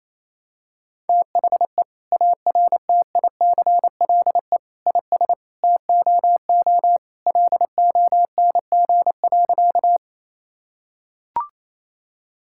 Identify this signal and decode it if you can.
Morse code, heard as a keyed tone, about 28 wpm, 700 Hz. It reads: THE ARTICLE IS TOO LONG.
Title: Sentences from Top 1000 Words - Review Part 5 - 28wpm